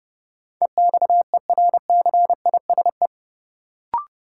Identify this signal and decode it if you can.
Morse code, heard as a keyed tone, about 30 wpm, 700 Hz. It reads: EXERCISE
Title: Sentences from Top 1000 Words - Review Part 4 - New Word Drill - 30wpm